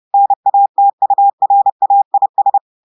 Morse code, heard as a keyed tone, about 30 wpm, 800 Hz. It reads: NATURAIS
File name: portugais texte en clair 2 vitesse 30 mots minutes